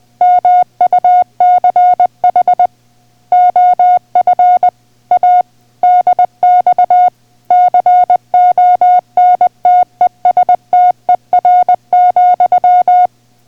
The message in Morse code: MUCH OF A DX CONTESTER,